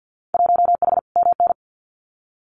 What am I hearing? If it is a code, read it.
Morse code, heard as a keyed tone, about 48 wpm, 700 Hz. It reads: 1H MN
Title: Winter Field Day Exchanges 50wpm